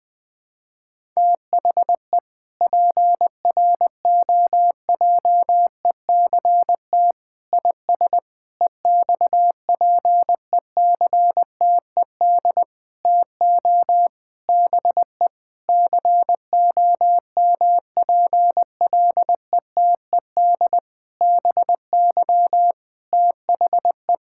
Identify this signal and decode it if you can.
Morse code, heard as a keyed tone, about 20 wpm, 700 Hz. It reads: THE PROJECT IS EXPECTED TO BE COMPLETED BY THE
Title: Sentences with 8-Word Phrases - Spoken Beginning - 20wpm